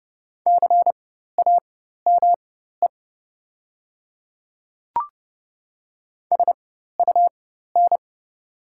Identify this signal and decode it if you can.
Morse code, heard as a keyed tone, about 30 wpm, spaced as for 14 wpm, 700 Hz. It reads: CAME SUN
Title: 4x Character Spacing - Top 300 Words - 30wpm